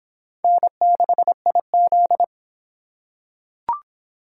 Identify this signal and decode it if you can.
Morse code, heard as a keyed tone, about 26 wpm, 700 Hz. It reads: N6IZ